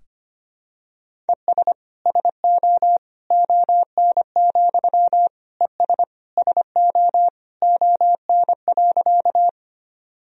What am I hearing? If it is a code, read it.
Morse code, heard as a keyed tone, about 25 wpm, 700 Hz. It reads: ES SO ON, ES SO ON.
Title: Sentences from Top 100 Words and Abbreviations 25wpm